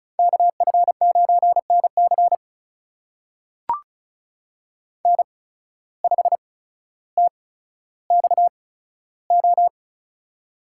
Morse code, heard as a keyed tone, about 35 wpm, spaced as for 15 wpm, 700 Hz. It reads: KF9NC N5TXO
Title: USA Call Signs - 5 Characters - 8x Character Spacing - 35wpm